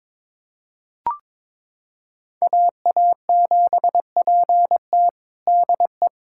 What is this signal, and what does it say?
Morse code, heard as a keyed tone, about 22 wpm, 700 Hz. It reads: AA7PT DE
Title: USA Callsign DE Callsign - 5 Characters - 22wpm